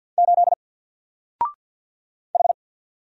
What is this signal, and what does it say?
Morse code, heard as a keyed tone, about 50 wpm, 700 Hz. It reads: 8 H
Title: Single Letter-Number 50wpm